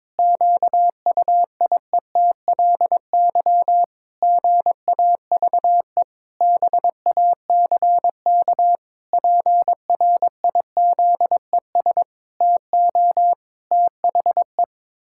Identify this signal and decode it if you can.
Morse code, heard as a keyed tone, about 22 wpm, 700 Hz. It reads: QUIETLY GAVE BACK PRIZES TO THE
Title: Pangram Challenge v2 - 22wpm